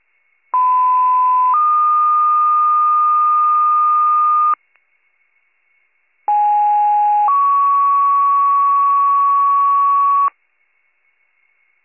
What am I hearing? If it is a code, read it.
Two-tone page, heard as a one-second tone, then a three-second tone, about 1010 then 1220 Hz, about 830 then 1120 Hz. It is heard from 0.5 to 10.3 s.